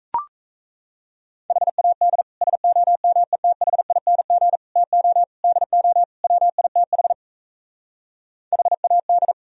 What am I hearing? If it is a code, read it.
Morse code, heard as a keyed tone, about 37 wpm, 700 Hz. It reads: HAD SOMETHING TO DO WITH HAD